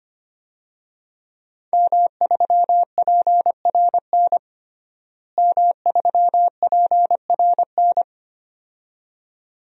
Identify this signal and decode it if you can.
Morse code, heard as a keyed tone, about 25 wpm, 700 Hz. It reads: M3PRN M3PRN
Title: DXCC-100 Call Signs - 5 Characters - Repeated 2x - 25wpm